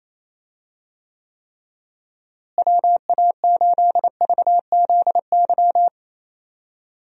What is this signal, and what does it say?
Morse code, heard as a keyed tone, about 28 wpm, 700 Hz. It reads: WA8VZY
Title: USA Call Signs - 6 Characters - Repeated 3x - 28wpm